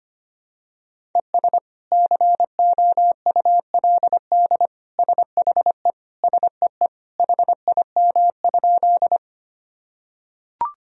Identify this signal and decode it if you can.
Morse code, heard as a keyed tone, about 25 wpm, 700 Hz. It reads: ES COULD SHE SEE HIM?
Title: Sentences from Top 100 Words and Abbreviations 25wpm